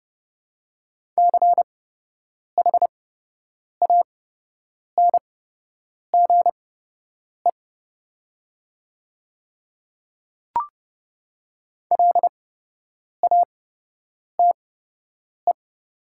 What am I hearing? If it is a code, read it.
Morse code, heard as a keyed tone, about 30 wpm, spaced as for 8 wpm, 700 Hz. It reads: CHANGE LATE
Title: 8x Character Spacing - Top 500 Words - 30wpm